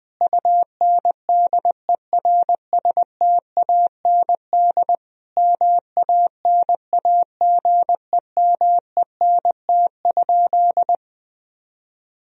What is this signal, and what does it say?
Morse code, heard as a keyed tone, about 19 wpm, 700 Hz. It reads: UNDERSTAND MANAGEMENT?
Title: Sentences from Top 1000 Words - Review Part 4 - 20wpm